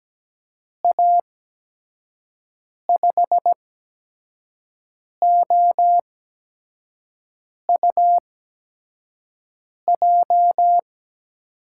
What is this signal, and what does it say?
Morse code, heard as a keyed tone, about 17 wpm, spaced as for 5 wpm, 700 Hz. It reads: A5OUJ
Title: USA Call Signs - 6 Characters - 8x Character Spacing - 17wpm